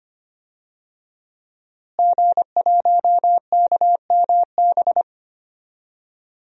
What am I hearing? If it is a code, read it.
Morse code, heard as a keyed tone, about 25 wpm, 700 Hz. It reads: G1KMB
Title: DXCC-100 Call Signs - 5 Characters - Repeated 2x - 25wpm